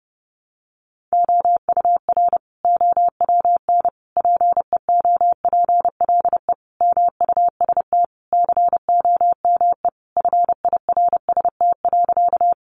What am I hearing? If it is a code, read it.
Morse code, heard as a keyed tone, about 30 wpm, 700 Hz. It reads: OUR OWN PEOPLE MUST COME FIRST.
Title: Sentences from Top 300 Words 30wpm